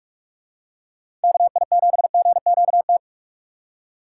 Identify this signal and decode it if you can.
Morse code, heard as a keyed tone, about 46 wpm, 700 Hz. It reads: KI7GQT
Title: USA Call Signs - 2x Character Spacing - 45wpm